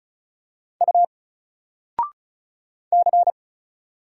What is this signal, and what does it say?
Morse code, heard as a keyed tone, about 33 wpm, 700 Hz. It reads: U C
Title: Single Letter-Number Alt 35wpm